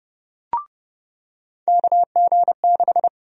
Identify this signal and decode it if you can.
Morse code, heard as a keyed tone, about 30 wpm, 700 Hz. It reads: KG6
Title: USA Call Sign Prefixes - 30wpm